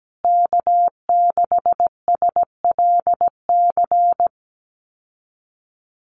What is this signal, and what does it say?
Morse code, heard as a keyed tone, about 17 wpm, 700 Hz. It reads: K6SLC
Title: USA Call Signs Repeated 2x 17wpm